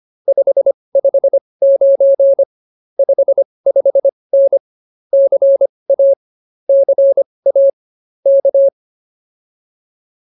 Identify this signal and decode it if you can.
Morse code, heard as a keyed tone, about 25 wpm, spaced as for 20 wpm, 590 Hz. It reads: 559 55N CA CA K